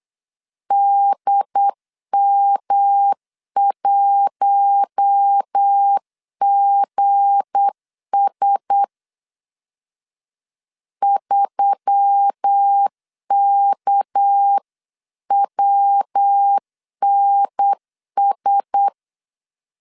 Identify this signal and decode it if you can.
Morse code, heard as a keyed tone, about 8 wpm, 800 Hz. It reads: DM1GS 3KWNS
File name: LEZIONI / 8 YJ45 / 8 YJ 8wpm 3313